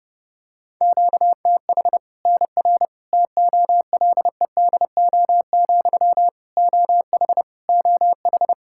Morse code, heard as a keyed tone, about 30 wpm, 700 Hz. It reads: QTH NR TOLEDO, OH OH
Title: Practice QSO - Exchange 1 - Beginning - 30wpm